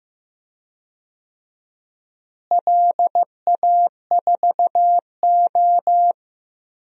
Morse code, heard as a keyed tone, about 15 wpm, 700 Hz. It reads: LA4O
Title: DXCC-100 Call Signs - 4 Characters - Repeated 3x - 15wpm